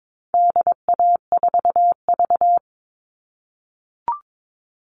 Morse code, heard as a keyed tone, about 22 wpm, 700 Hz. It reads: DA4V
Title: All Cut Numbers - 4-Digit 22wpm